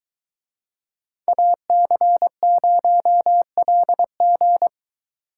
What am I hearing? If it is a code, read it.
Morse code, heard as a keyed tone, about 22 wpm, 700 Hz. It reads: AC0LG